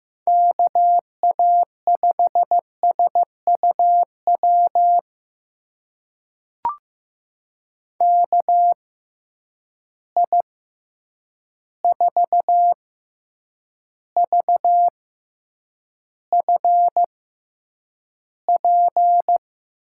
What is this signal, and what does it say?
Morse code, heard as a keyed tone, about 15 wpm, spaced as for 8 wpm, 700 Hz. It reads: KA5SUW KI4VFP
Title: USA Call Signs - 6 Characters - 6x Character Spacing - 15wpm